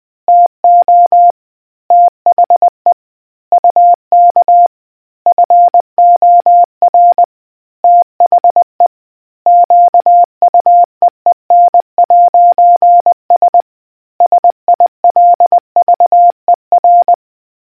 Morse code, heard as a keyed tone, about 20 wpm, 700 Hz. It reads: TO THE UK FOR THE QUEEN'S SILVER